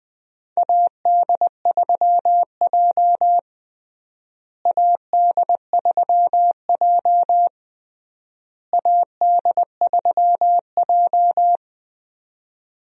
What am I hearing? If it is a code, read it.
Morse code, heard as a keyed tone, about 20 wpm, 700 Hz. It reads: AD3J AD3J AD3J
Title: USA Call Signs - 4 Characters - Repeated 3x - 20wpm